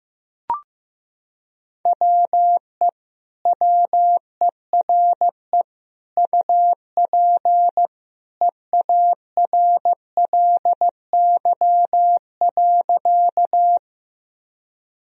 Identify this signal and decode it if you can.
Morse code, heard as a keyed tone, about 15 wpm, 700 Hz. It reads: WE WERE UP EARLY.
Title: Sentences from Top 1000 Words - Review Part 2 - 15wpm